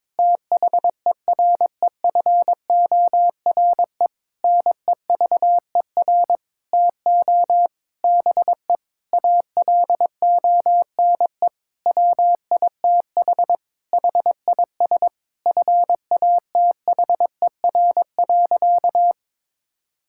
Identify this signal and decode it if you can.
Morse code, heard as a keyed tone, about 22 wpm, 700 Hz. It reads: THEREFORE NEVER TO BE ALONE WITH HIS FATHER.
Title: Sentences from Top 1000 Words - Review Part 5 - 22wpm